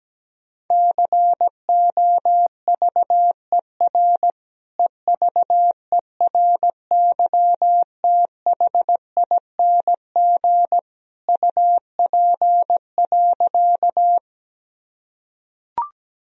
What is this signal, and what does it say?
Morse code, heard as a keyed tone, about 17 wpm, 700 Hz. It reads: COVER EVERYTHING UP.
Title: Sentences from Top 1000 Words - Review Part 1 - 17wpm